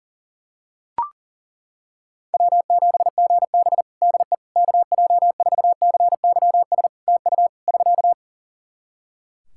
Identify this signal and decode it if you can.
Morse code, heard as a keyed tone, about 42 wpm, 700 Hz. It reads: W7GB DE KJ4CYS TU <SK>
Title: Practice QSO - Exchange 4 - Ending - 40wpm